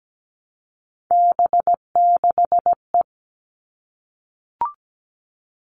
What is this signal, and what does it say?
Morse code, heard as a keyed tone, about 17 wpm, 700 Hz. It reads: B6E
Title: All Cut Numbers - 3-Digit 17wpm